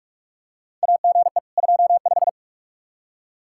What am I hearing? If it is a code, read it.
Morse code, heard as a keyed tone, about 46 wpm, 700 Hz. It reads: AGE 25
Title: Practice QSO - Exchange 3 - More Info - 45wpm